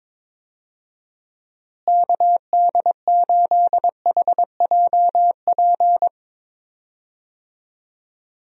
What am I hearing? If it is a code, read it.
Morse code, heard as a keyed tone, about 21 wpm, 700 Hz. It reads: KD8HJP
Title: USA Call Signs - 6 Characters - Repeated 2x - 22wpm